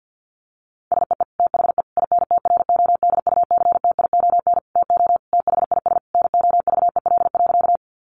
Morse code, heard as a keyed tone, about 50 wpm, 700 Hz. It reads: SEE THE INTRODUCTION TO THIS NOVEL.
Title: Sentences from Top 1300 Words 50wpm